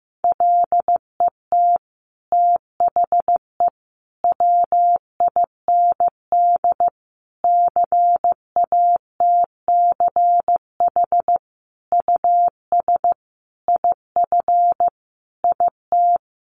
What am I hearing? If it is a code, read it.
Morse code, heard as a keyed tone, about 15 wpm, 700 Hz. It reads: LET THE WIND CATCH US IF IT
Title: Sentences from Top 1500 Words 15wpm